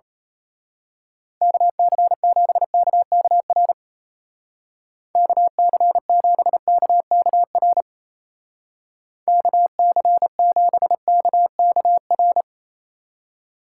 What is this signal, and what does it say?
Morse code, heard as a keyed tone, about 32 wpm, 700 Hz. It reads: KC7KKR KC7KKR KC7KKR
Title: Speed Racing - USA 2x3 Call Signs 28wpm